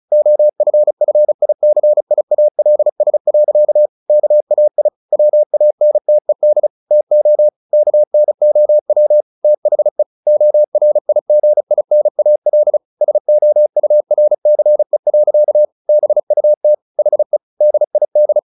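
Morse code, heard as a keyed tone, about 34 wpm, 600 Hz. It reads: OFFICIALS. KAI WANTED TO KNOW THE ORIGINAL SOURCE. BUT HE DID
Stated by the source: Morse practice at 35 wpm from CW med Gnister, sending Ken Follett's novel Never in English.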